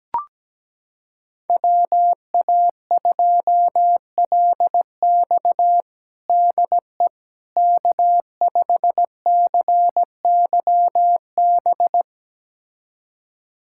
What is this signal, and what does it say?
Morse code, heard as a keyed tone, about 17 wpm, 700 Hz. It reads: WA2LX DE K5CYB
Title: USA Callsign DE Callsign - 5 Characters - 17wpm